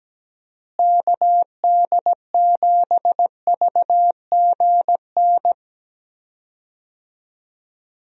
Morse code, heard as a keyed tone, about 17 wpm, 700 Hz. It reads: KD7VGN
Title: USA Call Signs - 6 Characters - Repeated 2x - 17wpm